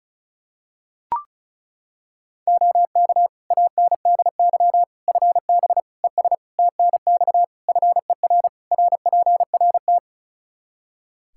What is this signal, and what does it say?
Morse code, heard as a keyed tone, about 36 wpm, 700 Hz. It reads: OK ANDY FB ES TNX FER RPRT
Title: K1USN SST QSO Practice 35wpm